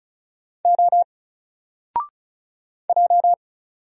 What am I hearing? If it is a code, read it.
Morse code, heard as a keyed tone, about 36 wpm, 700 Hz. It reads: O J
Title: Single Character (Letter:Number:Punctuation:Prosign) 35wpm